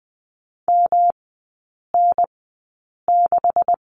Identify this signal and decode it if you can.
Morse code, heard as a keyed tone, about 20 wpm, spaced as for 8 wpm, 700 Hz. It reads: MN6
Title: 127 Lesson 21e - Number 6 - Three Characters 20wpm